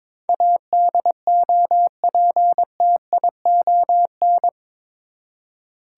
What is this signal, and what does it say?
Morse code, heard as a keyed tone, about 22 wpm, 700 Hz. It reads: ADOPTION
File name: practice / 22wpm / Eight Letter Words - 22wpm